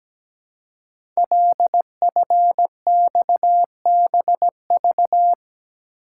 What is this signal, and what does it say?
Morse code, heard as a keyed tone, about 17 wpm, 700 Hz. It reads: LFXBV
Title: Code Groups - 5 Letters - 17wpm